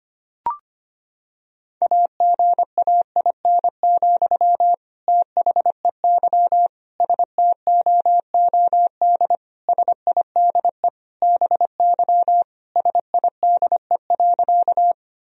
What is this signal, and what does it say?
Morse code, heard as a keyed tone, about 25 wpm, 700 Hz. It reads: AGAIN, THEY STOOD SIDE BY SIDE.
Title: Sentences from Top 1000 Words - Review Part 2 - 25wpm